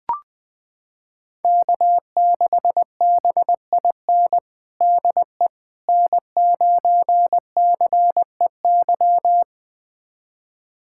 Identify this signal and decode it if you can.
Morse code, heard as a keyed tone, about 20 wpm, 700 Hz. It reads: K6BIN DE N9CEY
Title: USA Callsign DE Callsign - 5 Characters - 20wpm